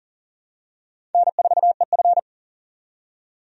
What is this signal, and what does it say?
Morse code, heard as a keyed tone, about 40 wpm, 700 Hz. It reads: N4EF
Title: USA Call Signs - 4 Characters - 40wpm